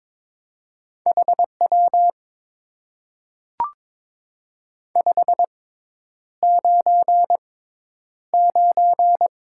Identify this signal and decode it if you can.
Morse code, heard as a keyed tone, about 22 wpm, spaced as for 10 wpm, 700 Hz. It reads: HW 599
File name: Top 90 QSO Elements - 6x Character Spacing - 22wpm